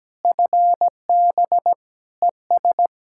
Morse code, heard as a keyed tone, about 17 wpm, 700 Hz. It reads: FB ES
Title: Practice QSO - Exchange 2 - Info - 17wpm